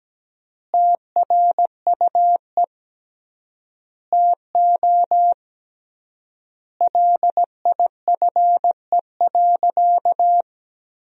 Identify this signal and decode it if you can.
Morse code, heard as a keyed tone, about 17 wpm, 700 Hz. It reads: TRUE TO LIFE.
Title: 3x Word-Spacing - Sentences from Top 400 Words - 17wpm